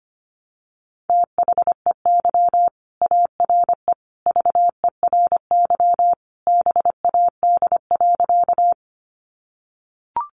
Morse code, heard as a keyed tone, about 25 wpm, 700 Hz. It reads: THEY ARE VERY BAD.
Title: Sentences from Top 700 Words 25wpm